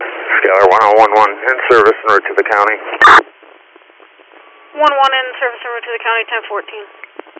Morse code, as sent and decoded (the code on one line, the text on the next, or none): none